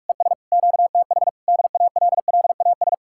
Morse code, heard as a keyed tone, about 59 wpm, spaced as for 52 wpm, 700 Hz. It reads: ES QTH DALLAS
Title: Speed-Racing Common QSO Phrases Encore - 45wpm